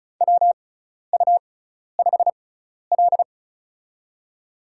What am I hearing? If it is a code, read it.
Morse code, heard as a keyed tone, about 34 wpm, spaced as for 12 wpm, 700 Hz. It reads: WU5L
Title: USA Call Signs - 6x Character Spacing - 35wpm